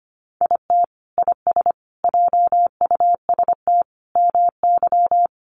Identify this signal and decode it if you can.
Morse code, heard as a keyed tone, about 25 wpm, 700 Hz. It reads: IT IS JUST MY
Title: Sentences from Top 600 Words 25wpm